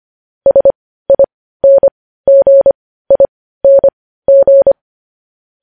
Morse code, heard as a keyed tone, about 25 wpm, spaced as for 15 wpm, 550 Hz. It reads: SINGING
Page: News Headlines in Morse Code at 15 WPM